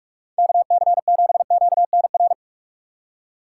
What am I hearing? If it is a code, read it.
Morse code, heard as a keyed tone, about 44 wpm, 700 Hz. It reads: KC7QNR
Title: USA Call Signs - 6 Characters - 6x Character Spacing - 45wpm